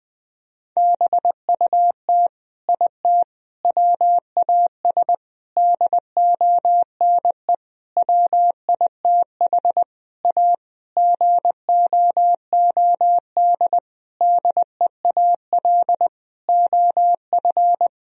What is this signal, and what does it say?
Morse code, heard as a keyed tone, about 20 wpm, 700 Hz. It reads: BUT IT WAS DONE WITH A GOOD DEAL OF
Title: Sentences from Top 1000 Words - Review Part 4 - 20wpm